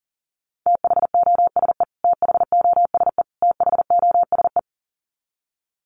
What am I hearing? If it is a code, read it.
Morse code, heard as a keyed tone, about 40 wpm, 700 Hz. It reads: THOSE THOSE THOSE
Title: Top 500 Words Repeated 40wpm